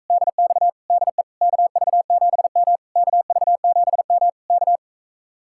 Morse code, heard as a keyed tone, about 43 wpm, 700 Hz. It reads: DX DE KV7M KV7M K